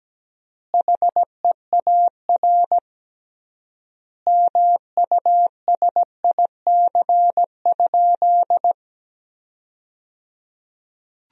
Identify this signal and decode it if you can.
Morse code, heard as a keyed tone, about 17 wpm, 700 Hz. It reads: HEAR MUSIC?
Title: 3x Word-Spacing - Sentences from Top 300 Words - 17wpm